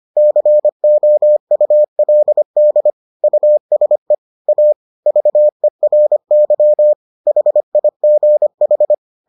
Morse code, heard as a keyed tone, about 25 wpm, 600 Hz. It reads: COULD USE A VERY HIGH